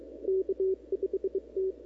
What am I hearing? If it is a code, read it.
Morse code, heard as a keyed tone, about 22 wpm, 380 Hz. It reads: K5T
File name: CW / 2005-k5tr arrl cw dx -17:00